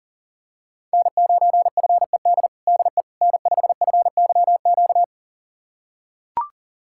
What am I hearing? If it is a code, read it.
Morse code, heard as a keyed tone, about 40 wpm, 700 Hz. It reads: N9FED DE N5FYQ
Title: USA Callsign DE Callsign - 5 Characters - 40wpm